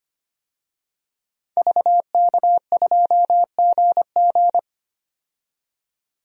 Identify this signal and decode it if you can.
Morse code, heard as a keyed tone, about 25 wpm, 700 Hz. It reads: VK2GG
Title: DXCC-100 Call Signs - 5 Characters - 25wpm